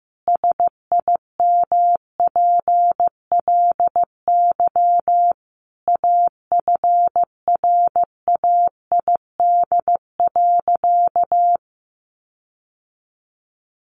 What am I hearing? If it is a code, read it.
Morse code, heard as a keyed tone, about 15 wpm, 700 Hz. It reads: SIMPLY AFRAID.
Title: Sentences from Top 1500 Words 15wpm